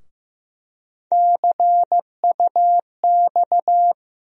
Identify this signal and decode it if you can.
Morse code, heard as a keyed tone, about 15 wpm, 700 Hz. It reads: CUX
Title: USA Call Sign Postfixes - 3 Letters 15wpm